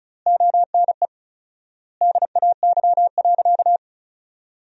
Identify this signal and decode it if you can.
Morse code, heard as a keyed tone, about 36 wpm, 700 Hz. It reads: ONE DAY.
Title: 4x Word-Spacing - Sentences from Top 300 Words - 35wpm